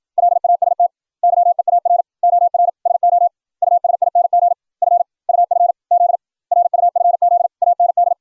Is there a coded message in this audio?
Yes: Morse code from 0.2 to 8.1 s, heard as a keyed tone, about 56 wpm, 700 Hz. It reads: LAST YEAR OR SO USING F UF Z RUFZ AND